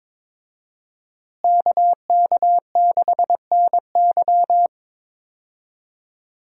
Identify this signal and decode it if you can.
Morse code, heard as a keyed tone, about 22 wpm, 700 Hz. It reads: KK6NY
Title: USA Call Signs - 5 Characters - Repeated 2x - 22wpm